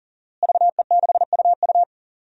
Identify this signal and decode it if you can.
Morse code, heard as a keyed tone, about 39 wpm, 700 Hz. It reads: VE6UU
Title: DXCC-100 Call Signs - 5 Characters - Repeated 2x - 40wpm